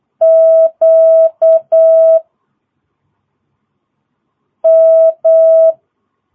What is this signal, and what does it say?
Morse code, heard as a keyed tone, about 8 wpm, 660 Hz. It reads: Q M
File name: jh8cbh / 2022cw